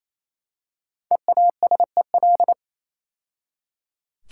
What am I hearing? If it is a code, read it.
Morse code, heard as a keyed tone, about 27 wpm, 700 Hz. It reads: EASEL
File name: Easy Words - Speed-Racing with Hints - 22wpm